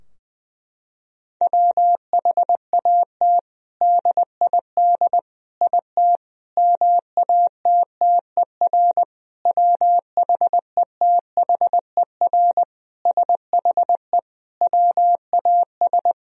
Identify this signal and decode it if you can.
Morse code, heard as a keyed tone, about 21 wpm, 700 Hz. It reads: WHAT DID IT MATTER WHETHER SHE WAS
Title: Sentences from Top 1000 Words - Review Part 3 - 20wpm